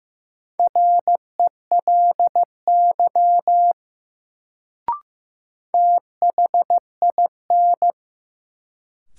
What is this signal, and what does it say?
Morse code, heard as a keyed tone, about 15 wpm, 700 Hz. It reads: RELY THIN
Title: Sentences from Top 1200 Words - New Word Drill - 15wpm